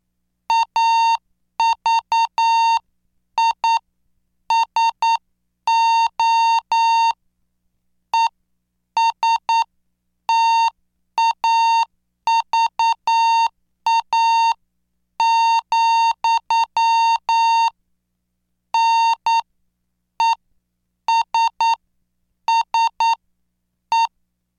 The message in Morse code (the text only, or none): AVISO ESTAVA, NESSE